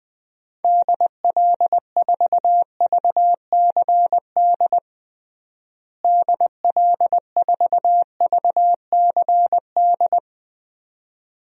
Morse code, heard as a keyed tone, about 20 wpm, 700 Hz. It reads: DL4VCD DL4VCD
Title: DXCC-100 Call Signs - 6 Characters - Repeated 2x - 20wpm